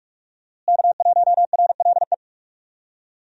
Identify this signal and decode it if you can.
Morse code, heard as a keyed tone, about 41 wpm, 700 Hz. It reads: K1RLE
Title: USA Call Signs - 5 Characters - Repeated 3x - 45wpm